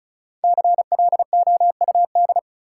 Morse code, heard as a keyed tone, about 36 wpm, 700 Hz. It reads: CLOUD
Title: Five Letter Words - 35wpm